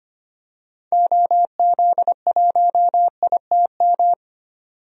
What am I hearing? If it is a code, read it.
Morse code, heard as a keyed tone, about 25 wpm, 700 Hz. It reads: OZ1ITM
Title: DXCC-100 Call Signs - 6 Characters - 25wpm